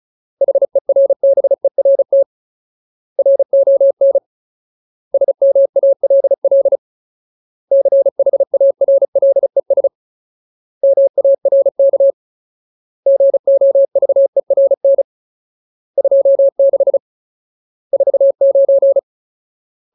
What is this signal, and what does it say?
Morse code, heard as a keyed tone, about 34 wpm, 550 Hz. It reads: HERBERT RON SMALL CHARLES MARK GOVERN 26 49
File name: Ultimate Word List / 35WPM Ultimate